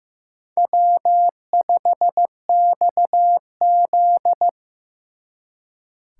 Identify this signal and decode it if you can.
Morse code, heard as a keyed tone, about 16 wpm, 700 Hz. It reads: W5XZ
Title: Speed Racing - USA 1x2 Call Signs 15wpm